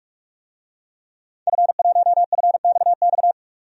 Morse code, heard as a keyed tone, about 46 wpm, 700 Hz. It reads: F1FXX
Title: DXCC-100 Call Signs - 5 Characters - 45wpm